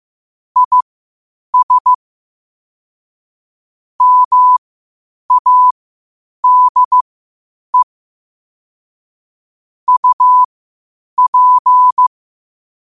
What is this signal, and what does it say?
Morse code, heard as a keyed tone, about 15 wpm, spaced as for 8 wpm, 1000 Hz. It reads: IS MADE UP